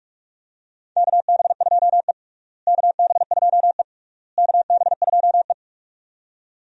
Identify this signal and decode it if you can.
Morse code, heard as a keyed tone, about 46 wpm, 700 Hz. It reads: KB2E KB2E KB2E
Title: USA Call Signs - 4 Characters - Repeated 3x - 45wpm